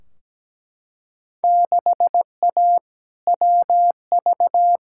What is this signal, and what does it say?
Morse code, heard as a keyed tone, about 17 wpm, 700 Hz. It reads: 6A WV
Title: Field Day Exchanges - 17wpm